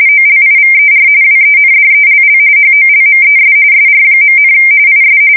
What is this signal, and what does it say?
RTTY, about 45.45 baud. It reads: OLIVER WENDELL HOLMES SR., IS
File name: fsk170 / usnavy1